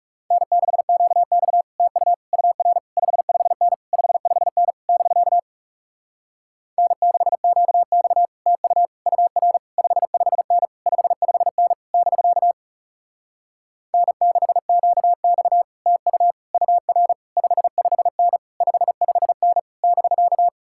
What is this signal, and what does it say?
Morse code, heard as a keyed tone, about 40 wpm, 700 Hz. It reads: N6QX TU UR 55N 55N <BK> N6QX TU UR 55N 55N <BK> N6QX TU UR 55N 55N <BK>